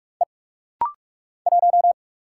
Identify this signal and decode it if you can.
Morse code, heard as a keyed tone, about 44 wpm, 700 Hz. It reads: E 1
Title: Single Letter-Number Alt - Rapid-Fire 45wpm